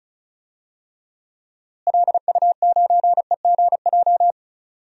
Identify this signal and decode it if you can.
Morse code, heard as a keyed tone, about 36 wpm, 700 Hz. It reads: LU9EGJ
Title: DXCC-100 Call Signs - 6 Characters - Repeated 2x - 35wpm